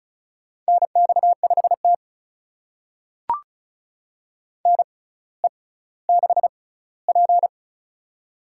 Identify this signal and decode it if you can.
Morse code, heard as a keyed tone, about 36 wpm, spaced as for 18 wpm, 700 Hz. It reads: NX5T NE6P
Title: USA Call Signs - 4 Characters - 6x Character Spacing - 35wpm